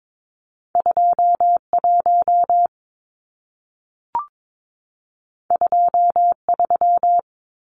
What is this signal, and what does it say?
Morse code, heard as a keyed tone, about 22 wpm, 700 Hz. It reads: 21 23
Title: Numbers - 2-Digit 22wpm